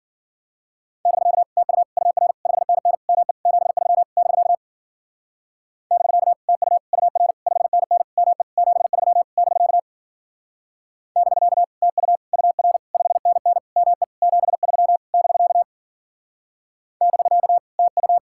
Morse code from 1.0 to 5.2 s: <BK> TU UR 5NN ME 73 <BK>

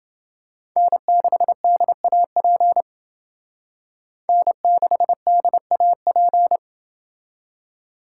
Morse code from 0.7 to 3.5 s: N6DAP

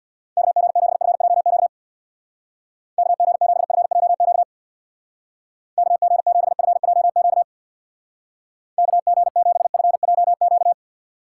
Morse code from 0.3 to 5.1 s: KG7FJQ KG7FJQ

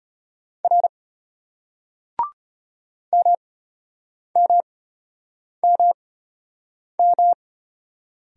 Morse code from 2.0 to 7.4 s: MMMM

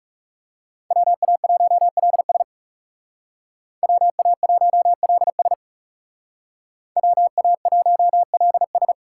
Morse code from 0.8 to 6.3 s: WA1LS WA1LS

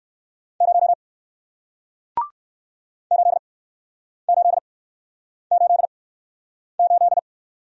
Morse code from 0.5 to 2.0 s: 0

